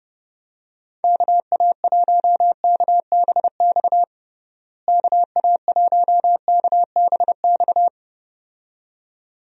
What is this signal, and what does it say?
Morse code, heard as a keyed tone, about 32 wpm, 700 Hz. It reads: KA1KBX KA1KBX
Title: USA Call Signs - 6 Characters - Repeated 2x - 30wpm